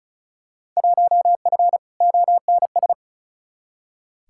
Morse code, heard as a keyed tone, about 35 wpm, 700 Hz. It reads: 1F ONS
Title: Field Day Exchanges Encore - 35wpm